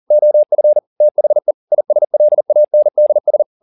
Morse code, heard as a keyed tone, about 40 wpm, 600 Hz. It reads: OF THE ISLANDS